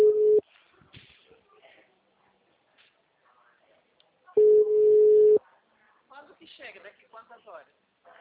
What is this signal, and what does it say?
Ringback tone (Europe, 425 Hz), with speech.